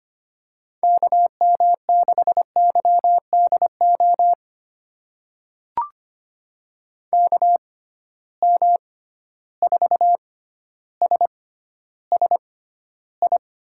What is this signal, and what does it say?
Morse code, heard as a keyed tone, about 25 wpm, spaced as for 13 wpm, 700 Hz. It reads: KM6YDO KM4SSI
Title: USA Call Signs - 6 Characters - 6x Character Spacing - 25wpm